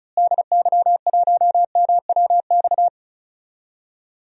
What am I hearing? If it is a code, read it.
Morse code, heard as a keyed tone, about 36 wpm, 700 Hz. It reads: DY1MWX
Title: DXCC-100 Call Signs - 6 Characters - Repeated 3x - 35wpm